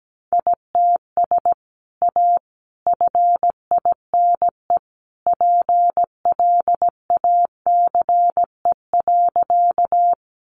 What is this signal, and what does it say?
Morse code, heard as a keyed tone, about 17 wpm, 700 Hz. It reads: ITS A FINE PLACE.
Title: Sentences from Top 600 Words 17wpm